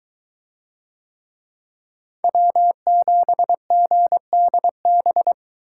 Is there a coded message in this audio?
Yes: Morse code from 2.2 to 5.3 s, heard as a keyed tone, about 22 wpm, 700 Hz. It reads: W7GDB